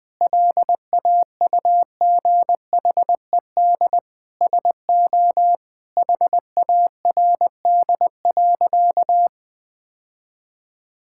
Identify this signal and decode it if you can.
Morse code, heard as a keyed tone, about 20 wpm, 700 Hz. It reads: LAUGHED SO HARD.
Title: Sentences with 7-Word Phrases - 20wpm